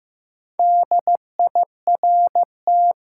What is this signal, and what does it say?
Morse code, heard as a keyed tone, about 15 wpm, 700 Hz. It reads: DIRT
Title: Four Letter Words - 15wpm